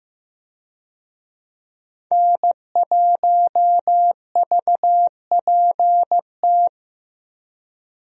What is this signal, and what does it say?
Morse code, heard as a keyed tone, about 15 wpm, 700 Hz. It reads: N1VPT
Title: USA Call Signs - 5 Characters - Repeated 3x - 15wpm